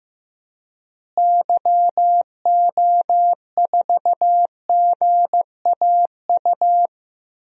Morse code, heard as a keyed tone, about 15 wpm, 700 Hz. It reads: YO4GAU
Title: DXCC-100 Call Signs - 6 Characters - Repeated 3x - 15wpm